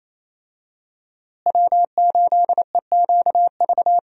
Morse code, heard as a keyed tone, about 28 wpm, 700 Hz. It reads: W8EQV
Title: USA Call Signs - 5 Characters - Repeated 3x - 28wpm